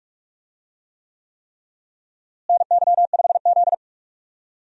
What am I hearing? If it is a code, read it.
Morse code, heard as a keyed tone, about 46 wpm, 700 Hz. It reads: NY5Z
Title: USA Call Signs - 4 Characters - Repeated 3x - 45wpm